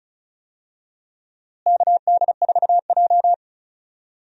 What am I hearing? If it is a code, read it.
Morse code, heard as a keyed tone, about 36 wpm, 700 Hz. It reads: KD4J
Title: USA Call Signs - 4 Characters - Repeated 2x - 35wpm